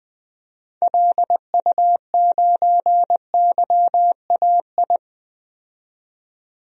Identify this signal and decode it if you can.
Morse code, heard as a keyed tone, about 20 wpm, 700 Hz. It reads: LU9YAI
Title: DXCC-100 Call Signs - 6 Characters - 20wpm